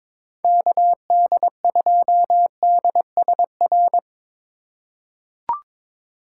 Morse code, heard as a keyed tone, about 22 wpm, 700 Hz. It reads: KD2DSR
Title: USA Call Signs - 6 Characters - 4x Character Spacing - 22wpm